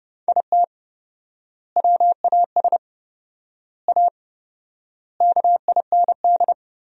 Morse code, heard as a keyed tone, about 30 wpm, 700 Hz. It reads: IT WAS A KIND
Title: 4x Word-Spacing - Sentences from Top 500 Words - 30wpm